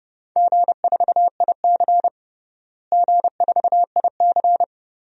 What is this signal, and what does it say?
Morse code, heard as a keyed tone, about 30 wpm, 700 Hz. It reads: G4IC G4IC
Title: DXCC-100 Call Signs - 4 Characters - Repeated 3x - 30wpm